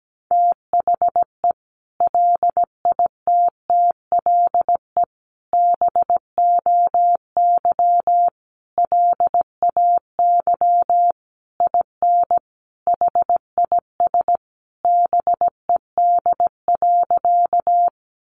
Morse code, heard as a keyed tone, about 17 wpm, 700 Hz. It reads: THE LITTLE BOY LAY IN HIS BED.
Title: Sentences from Top 500 Words 17wpm